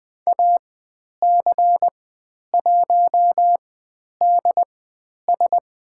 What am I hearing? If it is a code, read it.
Morse code, heard as a keyed tone, about 20 wpm, spaced as for 10 wpm, 700 Hz. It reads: AC1DS
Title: 109 Lesson 18g - Letter m - Call Signs 20wpm